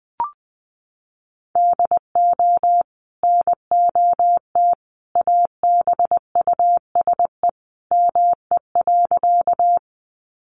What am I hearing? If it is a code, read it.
Morse code, heard as a keyed tone, about 20 wpm, 700 Hz. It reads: DO NOT ABUSE ME.
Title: Sentences from Top 1300 Words 20wpm